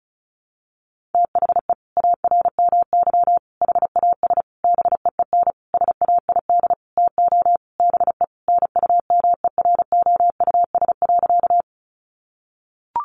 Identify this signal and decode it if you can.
Morse code, heard as a keyed tone, about 34 wpm, 700 Hz. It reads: THE ARMY HAS BEEN SAID TO BE NUMEROUS.
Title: Sentences from Top 1400 Words 35wpm